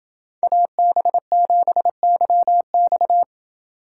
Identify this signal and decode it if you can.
Morse code, heard as a keyed tone, about 27 wpm, 700 Hz. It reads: AB7YX